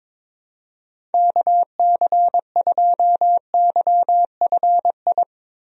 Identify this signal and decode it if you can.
Morse code, heard as a keyed tone, about 21 wpm, 700 Hz. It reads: KC2YFI
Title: USA Call Signs - 6 Characters - 22wpm